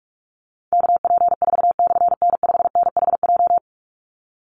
Morse code, heard as a keyed tone, about 44 wpm, 700 Hz. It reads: KP4/N5NHJ
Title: Call Signs - 45wpm